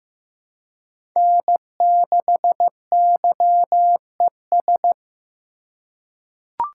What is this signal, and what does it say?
Morse code, heard as a keyed tone, about 15 wpm, 700 Hz. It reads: N6YES
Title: USA Call Signs - 3x Character Spacing - 15wpm